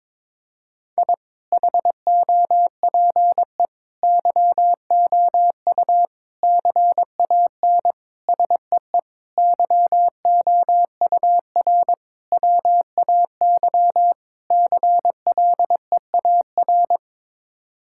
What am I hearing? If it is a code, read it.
Morse code, heard as a keyed tone, about 22 wpm, 700 Hz. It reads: I HOPE YOU CAN SEE YOUR WAY CLEAR
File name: practice / 22wpm / Sets of 8 Words - 22wpm